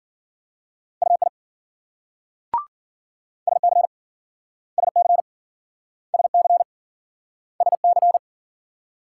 Morse code, heard as a keyed tone, about 54 wpm, 700 Hz. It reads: UI SC SC SC SC